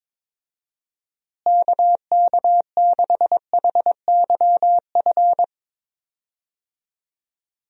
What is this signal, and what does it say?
Morse code, heard as a keyed tone, about 22 wpm, 700 Hz. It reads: KK6HYF
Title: USA Call Signs - 6 Characters - Repeated 3x - 22wpm